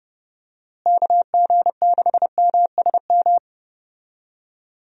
Morse code, heard as a keyed tone, about 31 wpm, 700 Hz. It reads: KG6MSM
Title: USA Call Signs - 6 Characters - Repeated 3x - 30wpm